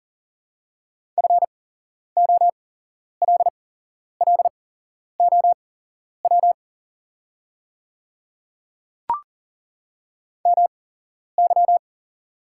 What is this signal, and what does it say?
Morse code, heard as a keyed tone, about 38 wpm, spaced as for 11 wpm, 700 Hz. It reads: FOLLOW MY
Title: 8x Character Spacing - Top 300 Words - 40wpm